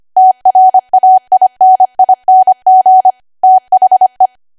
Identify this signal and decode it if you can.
Morse code, heard as a keyed tone, about 25 wpm, 750 Hz. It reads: TRAINING THE
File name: Archive / 25 WPM / 25WPM